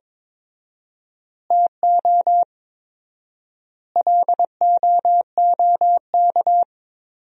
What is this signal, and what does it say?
Morse code, heard as a keyed tone, about 22 wpm, 700 Hz. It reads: TO LOOK
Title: 4x Word-Spacing - Sentences from Top 100 Words - 22wpm